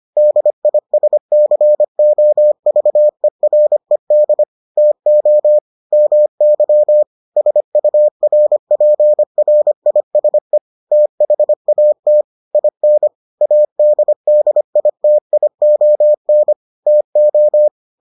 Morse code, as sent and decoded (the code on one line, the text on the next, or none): -.. .. ... -.-. --- ...- . .-. -.. / - --- / -- -.-- / ... ..- .-. .--. .-. .. ... / - .... .- - / .. -. / .- -.. -.. .. - .. --- -. / - ---
DISCOVERED TO MY SURPRISE THAT IN ADDITION TO